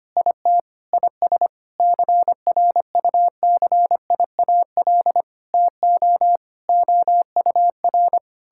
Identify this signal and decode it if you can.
Morse code, heard as a keyed tone, about 25 wpm, 700 Hz. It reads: IT IS CRUCIAL TO OUR
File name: practice / 25wpm / Sentences with 9-Word Phrases - 25wpm